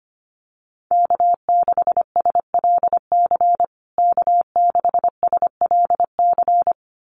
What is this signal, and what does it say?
Morse code, heard as a keyed tone, about 25 wpm, 700 Hz. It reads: K6SLC K6SLC
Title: USA Call Signs Repeated 2x 25wpm